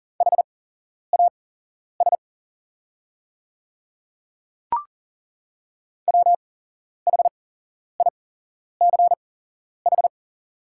Morse code, heard as a keyed tone, about 40 wpm, spaced as for 11 wpm, 700 Hz. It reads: HAS WHICH
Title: 8x Character Spacing - Top 100 Words - 40wpm